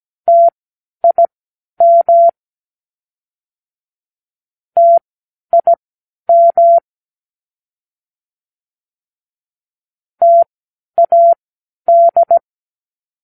Morse code, heard as a keyed tone, about 17 wpm, spaced as for 10 wpm, 680 Hz. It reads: TIM TIM TAD
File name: k7qo / Practicewords